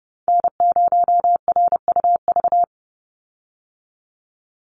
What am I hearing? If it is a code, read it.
Morse code, heard as a keyed tone, about 29 wpm, 700 Hz. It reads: N0RUV
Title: USA Call Signs 30wpm